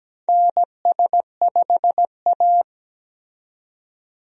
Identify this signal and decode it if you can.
Morse code, heard as a keyed tone, about 17 wpm, 700 Hz. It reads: NS5A
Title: USA Call Signs - 4 Characters - 17wpm